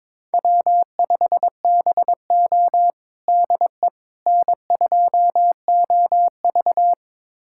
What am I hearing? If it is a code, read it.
Morse code, heard as a keyed tone, about 22 wpm, 700 Hz. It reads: W5BO DE N2OV
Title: USA Callsign DE Callsign - 4 Characters - 22wpm